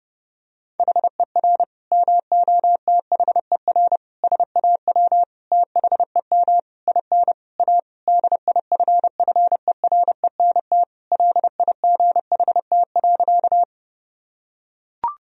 Morse code, heard as a keyed tone, about 32 wpm, 700 Hz. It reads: HER MOTHER SAW THEM IN A DIFFERENT LIGHT.
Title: Sentences from Top 1000 Words - Review Part 3 - 30wpm